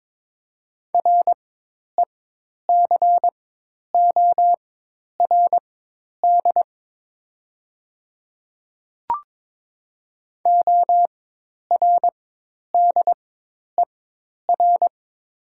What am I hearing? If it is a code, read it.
Morse code, heard as a keyed tone, about 22 wpm, spaced as for 10 wpm, 700 Hz. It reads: RECORD ORDER
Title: 4x Character Spacing - Top 500 Words - 22wpm